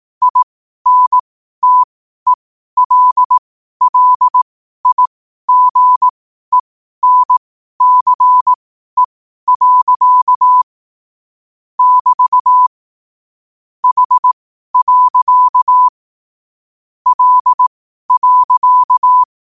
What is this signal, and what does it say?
Morse code, heard as a keyed tone, about 18 wpm, spaced as for 12 wpm, 1000 Hz. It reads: INTELLIGENCE. <BT> H. L.
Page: Quotes for Fri, 15 Aug 2025 in Morse Code at 12 words per minute.